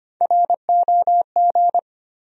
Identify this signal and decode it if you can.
Morse code, heard as a keyed tone, about 25 wpm, 700 Hz. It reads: ROG
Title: USA Common Names - Max 3 Letters - 25wpm